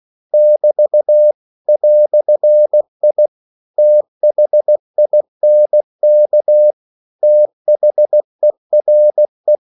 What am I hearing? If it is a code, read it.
Morse code, heard as a keyed tone, about 16 wpm, 600 Hz. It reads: <BT> "I THINK THERE
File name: CW / Practice / EnglishQuotations0416